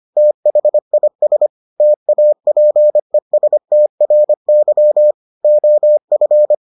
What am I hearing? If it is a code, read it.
Morse code, heard as a keyed tone, about 25 wpm, 600 Hz. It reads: THIS TAPESTRY OF